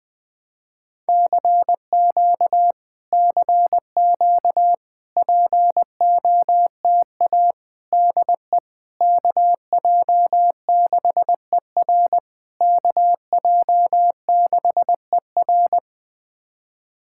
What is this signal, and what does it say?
Morse code, heard as a keyed tone, about 20 wpm, 690 Hz. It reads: CQ CQ POTA DE KJ6ER KJ6ER
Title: POTA QSO Practice - 20wpm